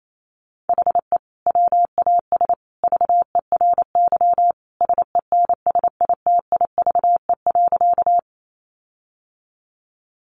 Morse code, heard as a keyed tone, about 28 wpm, 700 Hz. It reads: HE WAS VERY SENSITIVE.
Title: Sentences from Top 1400 Words 28wpm